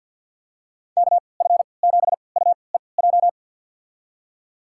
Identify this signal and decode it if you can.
Morse code, heard as a keyed tone, about 52 wpm, spaced as for 29 wpm, 700 Hz. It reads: KF7UEJ
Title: USA Call Signs - 6 Characters - 3x Character Spacing - 50wpm